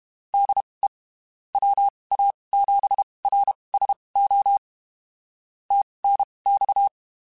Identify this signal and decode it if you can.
Morse code, heard as a keyed tone, about 32 wpm, spaced as for 23 wpm, 800 Hz. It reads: DE WA7RSO TNX
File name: (spaced) / QSOs Spaced-mp3s / kc7sac @30 sp